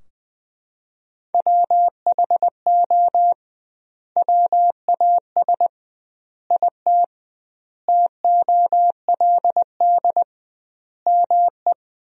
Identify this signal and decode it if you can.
Morse code, heard as a keyed tone, about 21 wpm, 700 Hz. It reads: WHO WAS IT TOLD ME
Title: 2x Word-Spacing - Sentences from Top 400 Words - 20wpm